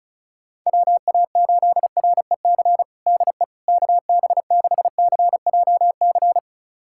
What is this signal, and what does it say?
Morse code, heard as a keyed tone, about 34 wpm, 700 Hz. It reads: WA8REC DE KB6CJC
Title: USA Callsign DE Callsign - 6 Characters - 35wpm